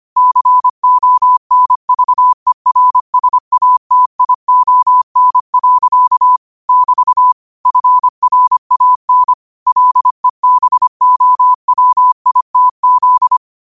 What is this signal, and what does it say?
Morse code, heard as a keyed tone, about 25 wpm, 1000 Hz. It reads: CONVERSATION. <BT> FRAN LEBOWITZ
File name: qotdfiles / 25wpm